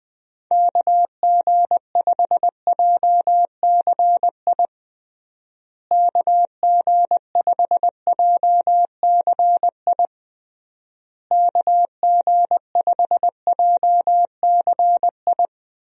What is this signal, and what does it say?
Morse code, heard as a keyed tone, about 21 wpm, 700 Hz. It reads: KG5JCI KG5JCI KG5JCI
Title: USA Call Signs - 6 Characters - Repeated 3x - 20wpm